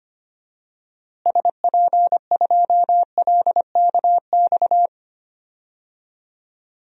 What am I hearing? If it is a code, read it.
Morse code, heard as a keyed tone, about 25 wpm, 700 Hz. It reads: SP2LKX
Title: DXCC-100 Call Signs - 6 Characters - Repeated 3x - 25wpm